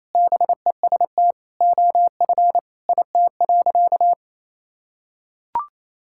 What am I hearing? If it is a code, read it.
Morse code, heard as a keyed tone, about 28 wpm, 700 Hz. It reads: BEST OF IT.
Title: Sentences from Top 1000 Words - Review Part 5 - 28wpm